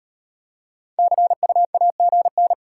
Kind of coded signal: Morse code

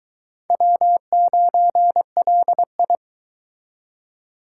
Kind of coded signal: Morse code

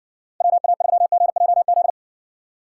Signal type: Morse code

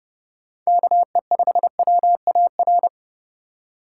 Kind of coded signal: Morse code

